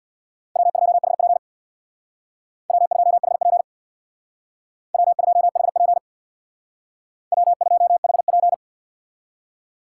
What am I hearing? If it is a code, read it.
Morse code, heard as a keyed tone, about 63 wpm, 700 Hz. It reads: W2HP W2HP W2HP W2HP